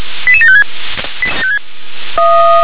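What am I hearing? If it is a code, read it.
Five-tone selective call (ZVEI), heard as quick runs of five tones, tones of about 80 ms. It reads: 88756
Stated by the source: ZVEI five-tone call 88690 received by SDR.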